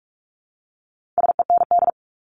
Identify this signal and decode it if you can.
Morse code, heard as a keyed tone, about 46 wpm, 700 Hz. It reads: SEND
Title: Multi-letter Phonemes Repeated 3x 45wpm